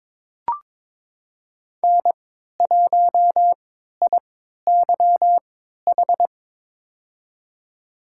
Morse code, heard as a keyed tone, about 22 wpm, spaced as for 12 wpm, 700 Hz. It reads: N1IYH
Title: USA Call Signs - 3x Character Spacing - 22wpm